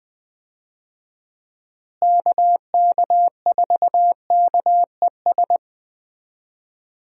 Morse code, heard as a keyed tone, about 21 wpm, 700 Hz. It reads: KK4KES
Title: USA Call Signs - 6 Characters - Repeated 2x - 20wpm